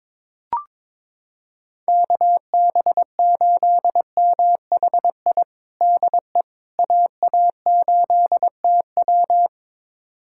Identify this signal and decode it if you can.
Morse code, heard as a keyed tone, about 22 wpm, 700 Hz. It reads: KB8MHI DE AA8TW
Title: USA Callsign DE Callsign 22wpm